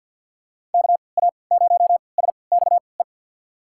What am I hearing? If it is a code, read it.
Morse code, heard as a keyed tone, about 46 wpm, spaced as for 27 wpm, 700 Hz. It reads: KA0SXE
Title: USA Call Signs - 6 Characters - 3x Character Spacing - 50wpm